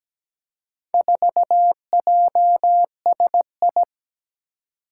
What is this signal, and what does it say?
Morse code, heard as a keyed tone, about 17 wpm, 700 Hz. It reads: 4JSI
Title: Four Challenging Letter-Number Combinations 17wpm